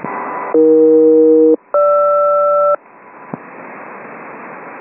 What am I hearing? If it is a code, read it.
Aircraft SELCAL, heard as two pairs of tones. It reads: AE-GR
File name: uploader / 27nov 6715 hunter21